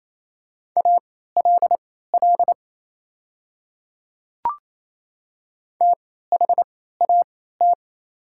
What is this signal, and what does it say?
Morse code, heard as a keyed tone, about 28 wpm, spaced as for 16 wpm, 700 Hz. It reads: ALL THAT
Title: 3x Character Spacing - Top 100 Words - 28wpm